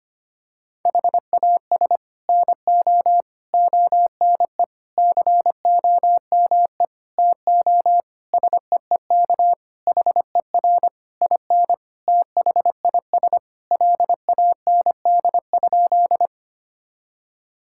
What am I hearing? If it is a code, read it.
Morse code, heard as a keyed tone, about 25 wpm, 700 Hz. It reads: HAS NO ONE COME TO SEEK HER IN THIS LAND?
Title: Sentences from Top 1600 Words 25wpm